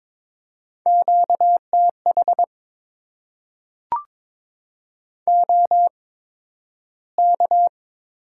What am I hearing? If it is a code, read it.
Morse code, heard as a keyed tone, about 22 wpm, spaced as for 12 wpm, 700 Hz. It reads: QTH OK
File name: Top 90 QSO Elements - 8x Character Spacing - 22wpm